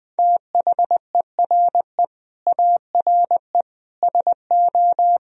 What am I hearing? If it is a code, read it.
Morse code, heard as a keyed tone, about 21 wpm, 700 Hz. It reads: THERE ARE SO
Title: Speed-Racing - Sets of 4 Words 20wpm